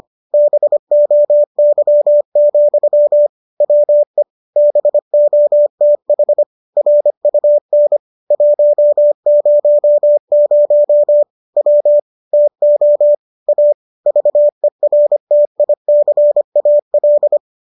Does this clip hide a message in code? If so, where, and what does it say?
Yes: Morse code from 0.3 to 17.4 s, heard as a keyed tone, about 25 wpm, 590 Hz. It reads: BOY, WE BOTH RUN 100 W TO A VERTICAL